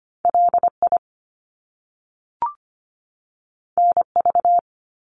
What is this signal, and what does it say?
Morse code, heard as a keyed tone, about 25 wpm, 700 Hz. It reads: LI NV
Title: Top Two Letter Combinations in Words 25wpm